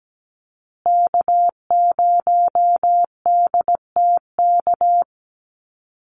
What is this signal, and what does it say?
Morse code, heard as a keyed tone, about 17 wpm, 700 Hz. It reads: K0DTK
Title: USA Call Signs Repeated 3x 17wpm